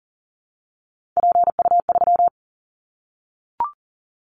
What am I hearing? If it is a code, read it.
Morse code, heard as a keyed tone, about 42 wpm, 700 Hz. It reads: PU3
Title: Callsign Prefixes for Top 100 DXCC - 40wpm